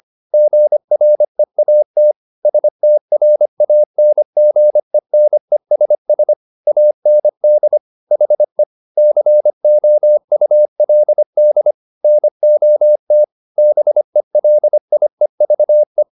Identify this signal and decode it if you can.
Morse code, heard as a keyed tone, about 25 wpm, 600 Hz. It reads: GREAT STRANGENESS AND HE COULD NOT BELIEVE